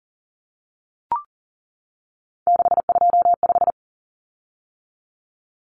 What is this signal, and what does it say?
Morse code, heard as a keyed tone, about 40 wpm, 700 Hz. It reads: 625
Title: Common Cut Numbers - 3-Digit 40wpm